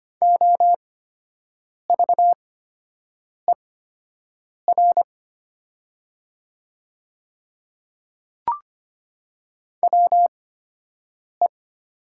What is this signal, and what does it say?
Morse code, heard as a keyed tone, about 25 wpm, spaced as for 7 wpm, 700 Hz. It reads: OVER WE
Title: 8x Character Spacing - Top 200 Words - 25wpm